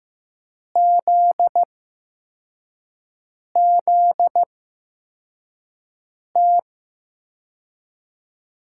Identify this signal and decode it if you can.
Morse code, heard as a keyed tone, about 15 wpm, spaced as for 4 wpm, 700 Hz. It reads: ZZT
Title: USA Call Signs - 6 Characters - 8x Character Spacing - 15wpm